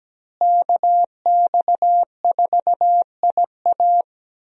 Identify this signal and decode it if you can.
Morse code, heard as a keyed tone, about 17 wpm, 700 Hz. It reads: KX4IA
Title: USA Call Signs - 5 Characters - Repeated 3x - 17wpm